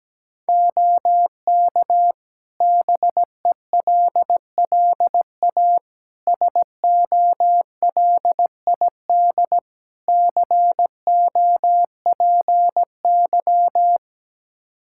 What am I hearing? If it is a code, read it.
Morse code, heard as a keyed tone, about 17 wpm, 700 Hz. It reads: OK BELLA SOLID COPY
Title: K1USN SST QSO Practice 17wpm